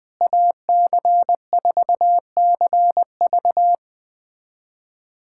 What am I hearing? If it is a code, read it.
Morse code, heard as a keyed tone, about 20 wpm, 700 Hz. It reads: AC4CV
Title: USA Callsign DE Callsign 20wpm